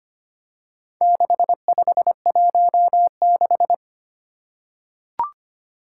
Morse code, heard as a keyed tone, about 25 wpm, 700 Hz. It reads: 6516
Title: Four Challenging Letter-Number Combinations 25wpm